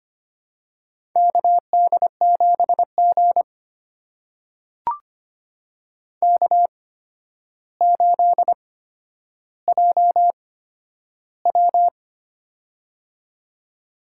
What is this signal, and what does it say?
Morse code, heard as a keyed tone, about 25 wpm, spaced as for 11 wpm, 700 Hz. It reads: KD7G K8JW
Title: USA Call Signs - 8x Character Spacing - 25wpm